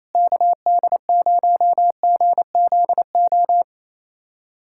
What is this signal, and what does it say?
Morse code, heard as a keyed tone, about 28 wpm, 700 Hz. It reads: KD0GZO
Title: USA Callsign DE Callsign 28wpm